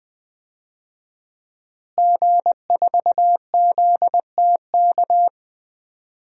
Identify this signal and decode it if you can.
Morse code, heard as a keyed tone, about 20 wpm, 700 Hz. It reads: G4ZTK
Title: DXCC-100 Call Signs - 5 Characters - Repeated 2x - 20wpm